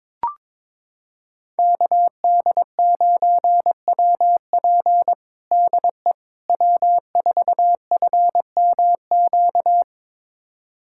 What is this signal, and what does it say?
Morse code, heard as a keyed tone, about 21 wpm, 700 Hz. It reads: KD9WP DE W4FMQ
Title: USA Callsign DE Callsign - 5 Characters - 22wpm